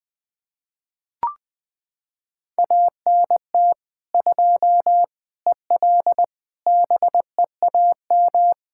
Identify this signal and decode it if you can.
Morse code, heard as a keyed tone, about 20 wpm, 700 Hz. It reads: ANT 2 EL BEAM
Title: Practice QSO - Exchange 2 - Info - 20wpm